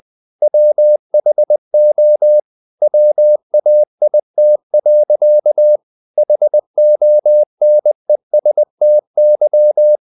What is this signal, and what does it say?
Morse code, heard as a keyed tone, about 20 wpm, 600 Hz. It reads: WHO WAIT. HONESTY
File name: CW / Practice / EnglishSayings20